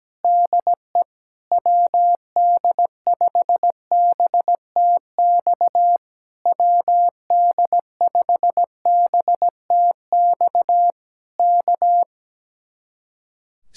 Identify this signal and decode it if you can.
Morse code, heard as a keyed tone, about 17 wpm, 700 Hz. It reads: DE WD5BTX WD5BTX K
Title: Speed-Racing Common QSO Phrases - 17wpm